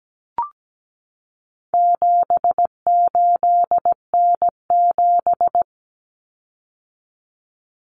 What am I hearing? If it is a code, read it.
Morse code, heard as a keyed tone, about 17 wpm, 700 Hz. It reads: 78N7
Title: Common Cut Numbers - 4-Digit 17wpm